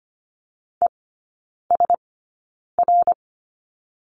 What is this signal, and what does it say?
Morse code, heard as a keyed tone, about 25 wpm, spaced as for 9 wpm, 700 Hz. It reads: ESR
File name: course / 25:10wpm / 093 Lesson 16e - Letter c - Three Characters 25wpm